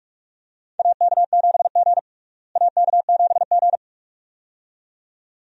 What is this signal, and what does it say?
Morse code, heard as a keyed tone, about 41 wpm, 700 Hz. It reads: AK7G AK7G
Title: USA Call Signs - 4 Characters - Repeated 2x - 45wpm